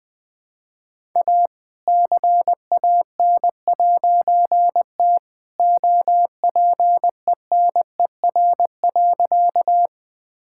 Morse code, heard as a keyed tone, about 20 wpm, 700 Hz. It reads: A CAN'T OPENER.
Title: All-Age Appropriate Jokes - 20wpm